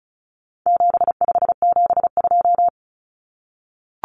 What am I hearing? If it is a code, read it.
Morse code, heard as a keyed tone, about 34 wpm, 700 Hz. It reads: 7572
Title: Numbers - 4-Digit 35wpm